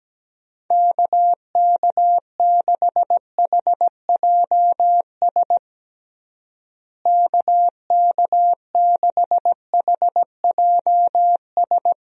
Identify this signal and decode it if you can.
Morse code, heard as a keyed tone, about 17 wpm, 700 Hz. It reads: KK6HJS KK6HJS
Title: USA Call Signs - 6 Characters - Repeated 2x - 17wpm